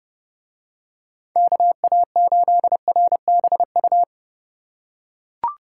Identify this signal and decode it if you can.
Morse code, heard as a keyed tone, about 27 wpm, 700 Hz. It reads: KA8RBU